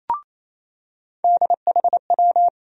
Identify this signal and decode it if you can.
Morse code, heard as a keyed tone, about 28 wpm, 700 Hz. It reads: DHW
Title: USA Call Sign Postfixes - 3 Letters 28wpm